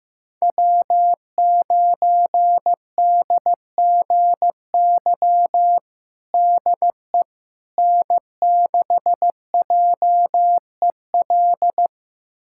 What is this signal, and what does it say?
Morse code, heard as a keyed tone, about 15 wpm, 700 Hz. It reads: W9DGY DE N6JEL
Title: USA Callsign DE Callsign - 5 Characters - 15wpm